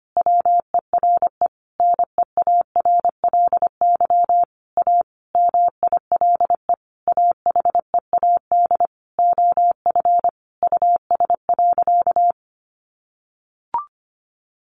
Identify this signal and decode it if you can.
Morse code, heard as a keyed tone, about 25 wpm, 700 Hz. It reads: WERE NEARLY A MILE AHEAD OF US.
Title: Sentences from Top 1300 Words 25wpm